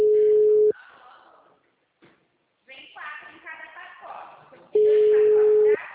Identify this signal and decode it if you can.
Ringback tone (Europe, 425 Hz), with speech.